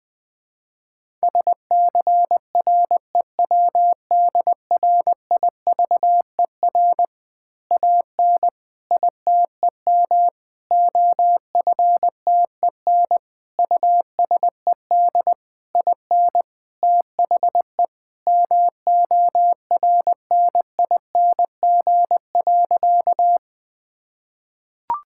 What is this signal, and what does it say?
Morse code, heard as a keyed tone, about 20 wpm, 700 Hz. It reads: SCREWDRIVER AN ITEM OFTEN USED IN THE MORNING.
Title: Easy Words with Hints - 20wpm